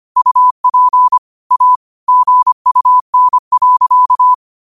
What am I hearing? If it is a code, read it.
Morse code, heard as a keyed tone, about 25 wpm, 1000 Hz. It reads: UP A GUN.
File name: 25wpm